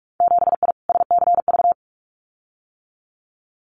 Morse code, heard as a keyed tone, about 46 wpm, 700 Hz. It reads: 7I SCV
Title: Winter Field Day Exchanges Encore 45wpm